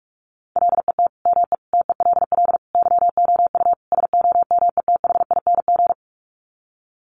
Morse code, heard as a keyed tone, about 44 wpm, 700 Hz. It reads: LET ME TELL YOU SOMETHING
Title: Sets of 5 Words 45wpm